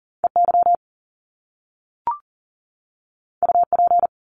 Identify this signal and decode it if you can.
Morse code, heard as a keyed tone, about 39 wpm, 700 Hz. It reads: EY UP
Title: Top Two Letter Combinations in Words 40wpm